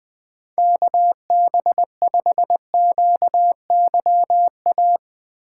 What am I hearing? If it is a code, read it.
Morse code, heard as a keyed tone, about 20 wpm, 700 Hz. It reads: KB5QYA
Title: USA Call Signs - 6 Characters - 2x Character Spacing - 20wpm